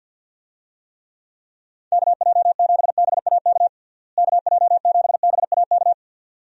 Morse code, heard as a keyed tone, about 49 wpm, 700 Hz. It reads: KJ7BAK KJ7BAK
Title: USA Call Signs - 6 Characters - Repeated 2x - 50wpm